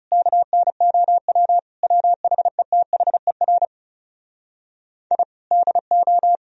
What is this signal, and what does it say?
Morse code, heard as a keyed tone, about 37 wpm, 700 Hz. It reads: KNOW WHETHER I DO
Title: Speed-Racing - Sets of 5 Words 30wpm